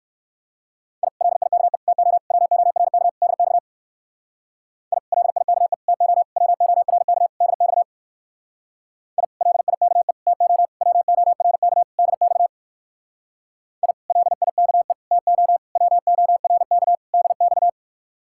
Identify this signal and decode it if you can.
Morse code, heard as a keyed tone, about 59 wpm, 700 Hz. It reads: I LIKE TO WORK DX I LIKE TO WORK DX I LIKE TO WORK DX I LIKE TO WORK DX